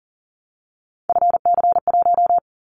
Morse code, heard as a keyed tone, about 40 wpm, 700 Hz. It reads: FC1
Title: Callsign Prefixes for Top 50 DXCC - 40wpm